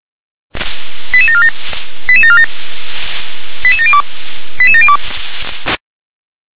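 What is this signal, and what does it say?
Five-tone selective call (ZVEI), heard as quick runs of five tones, tones of about 70 ms. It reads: 88747 88747 88702 88702
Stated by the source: ZVEI five-tone call 88080 received by SDR.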